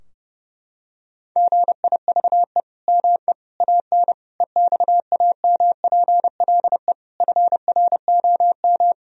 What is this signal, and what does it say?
Morse code, heard as a keyed tone, about 32 wpm, 700 Hz. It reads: GIVE ME AN EXAMPLE FROM
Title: Sentences from Top 1000 Words - Review Part 3 - 30wpm